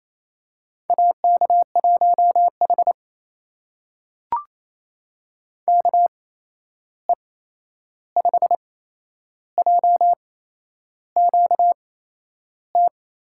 Morse code, heard as a keyed tone, about 28 wpm, spaced as for 11 wpm, 700 Hz. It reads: AK1H KE5JQT
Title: USA Call Signs - 8x Character Spacing - 28wpm